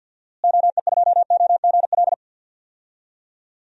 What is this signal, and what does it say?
Morse code, heard as a keyed tone, about 51 wpm, 700 Hz. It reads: OE2OGL
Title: DXCC-100 Call Signs - 6 Characters - Repeated 3x - 50wpm